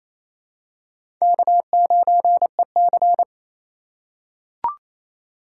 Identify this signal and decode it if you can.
Morse code, heard as a keyed tone, about 28 wpm, 690 Hz. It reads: K9EC
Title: USA Call Signs - 2x Character Spacing - 28wpm